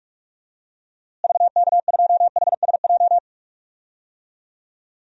Morse code, heard as a keyed tone, about 46 wpm, 700 Hz. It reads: VK2HSJ
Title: DXCC-100 Call Signs - 6 Characters - Repeated 3x - 45wpm